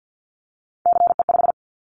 Morse code, heard as a keyed tone, about 50 wpm, 700 Hz. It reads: CE5
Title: Callsign Prefixes for Top 100 DXCC - 50wpm